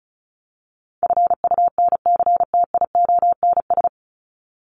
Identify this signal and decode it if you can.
Morse code, heard as a keyed tone, about 34 wpm, 700 Hz. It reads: FUNCTIONS
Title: ES:S Suffix - Top 100 Words with Base Words 35wpm